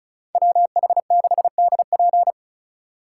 Morse code, heard as a keyed tone, about 47 wpm, spaced as for 41 wpm, 700 Hz. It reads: WH6DP
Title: Speed Racing - USA 2x2 Call Signs 35wpm